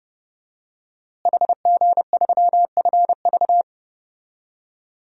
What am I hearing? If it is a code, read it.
Morse code, heard as a keyed tone, about 32 wpm, 700 Hz. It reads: HG3FV
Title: DXCC-100 Call Signs - 5 Characters - Repeated 3x - 30wpm